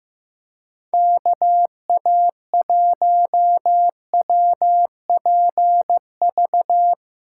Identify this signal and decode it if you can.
Morse code, heard as a keyed tone, about 15 wpm, 700 Hz. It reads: KA1WPV
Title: USA Callsign DE Callsign - 6 Characters - 15wpm